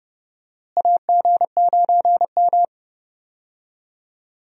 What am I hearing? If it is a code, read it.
Morse code, heard as a keyed tone, about 30 wpm, 700 Hz. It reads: AG9M
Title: USA Call Signs - 4 Characters - Repeated 2x - 30wpm